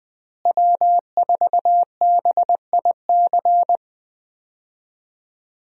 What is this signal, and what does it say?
Morse code, heard as a keyed tone, about 21 wpm, 700 Hz. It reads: W4BIC
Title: USA Call Signs - 5 Characters - Repeated 3x - 20wpm